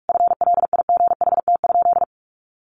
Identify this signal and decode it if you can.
Morse code, heard as a keyed tone, about 44 wpm, 700 Hz. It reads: FLIGHT?
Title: Sentences from Top 1000 Words 45wpm